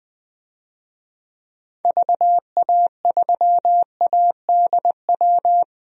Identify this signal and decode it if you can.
Morse code, heard as a keyed tone, about 21 wpm, 700 Hz. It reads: VA3ADW
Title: DXCC-100 Call Signs - 6 Characters - 20wpm